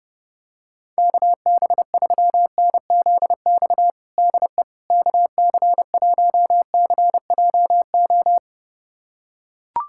Morse code, heard as a keyed tone, about 30 wpm, 700 Hz. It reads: KB3NZX DE KC1CJO
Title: USA Callsign DE Callsign 30wpm